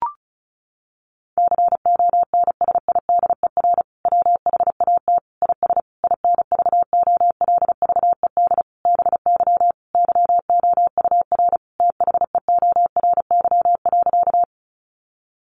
Morse code, heard as a keyed tone, about 34 wpm, 700 Hz. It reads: CONSIDER WHAT IS INVOLVED BY YOUR THEORY.
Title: Sentences from Top 800 Words 35wpm